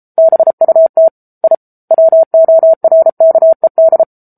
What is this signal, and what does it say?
Morse code, heard as a keyed tone, about 34 wpm, 650 Hz. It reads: BUT I WORKED
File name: Book Two / 07-Chapter 7 - 33 WPM